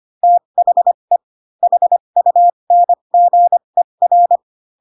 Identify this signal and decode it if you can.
Morse code, heard as a keyed tone, about 25 wpm, spaced as for 22 wpm, 700 Hz. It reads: THE HUNGER
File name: Pinocchio / Pinocchio ch6 18 25 ik1hge